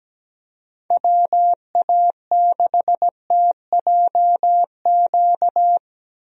Morse code, heard as a keyed tone, about 17 wpm, 700 Hz. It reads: WA6TJQ
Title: USA Call Signs - 6 Characters - Repeated 3x - 17wpm